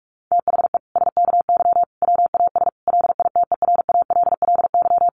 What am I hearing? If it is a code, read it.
Morse code, heard as a keyed tone, about 44 wpm, 700 Hz. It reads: THE SKY WAS LITERALLY
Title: Sentences from Top 1400 Words 45wpm